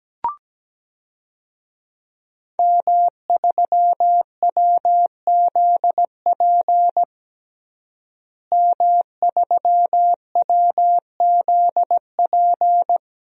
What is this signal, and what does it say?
Morse code, heard as a keyed tone, about 17 wpm, 700 Hz. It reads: M3WZP M3WZP
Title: DXCC-100 Call Signs - 5 Characters - Repeated 2x - 17wpm